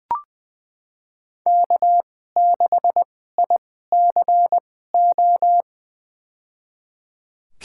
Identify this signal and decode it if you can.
Morse code, heard as a keyed tone, about 20 wpm, spaced as for 14 wpm, 700 Hz. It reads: K6ICO
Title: USA Call Signs - 5 Characters - 2x Character Spacing - 20wpm